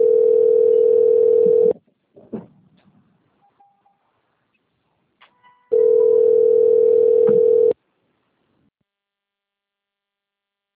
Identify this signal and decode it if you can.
Ringback tone (North America), 440 + 480 Hz.